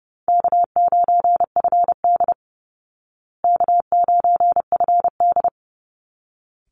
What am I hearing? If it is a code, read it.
Morse code, heard as a keyed tone, about 30 wpm, 700 Hz. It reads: K9FD K9FD
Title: Easy Call Signs Repeated 2x - 30wpm